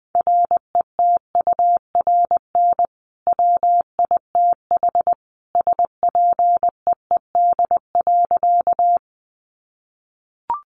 Morse code, heard as a keyed tone, about 20 wpm, 700 Hz. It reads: RETURN WITH SPEED.
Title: Sentences from Top 800 Words 20wpm